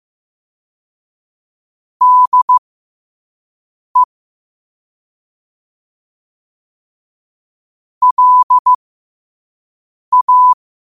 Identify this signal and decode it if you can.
Morse code, heard as a keyed tone, about 15 wpm, spaced as for 5 wpm, 1000 Hz. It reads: DE LA
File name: qotdfiles / 5wpm